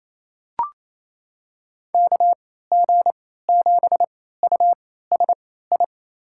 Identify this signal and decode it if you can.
Morse code, heard as a keyed tone, about 28 wpm, spaced as for 16 wpm, 700 Hz. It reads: KG7USI
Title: USA Call Signs - 6 Characters - 3x Character Spacing - 28wpm